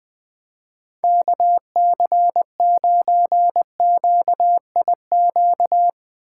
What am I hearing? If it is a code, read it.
Morse code, heard as a keyed tone, about 20 wpm, 700 Hz. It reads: KC9QIQ
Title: Speed Racing - USA 2x3 Call Signs 17wpm